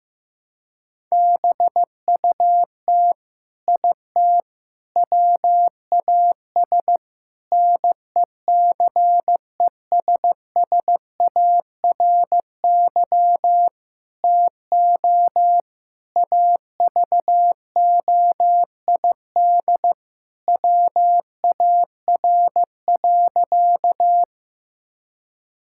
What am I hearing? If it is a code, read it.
Morse code, heard as a keyed tone, about 16 wpm, 700 Hz. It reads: BUT IT WAS NECESSARY TO AVOID WAR.
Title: Sentences from Top 1000 Words - Review Part 6 - 15wpm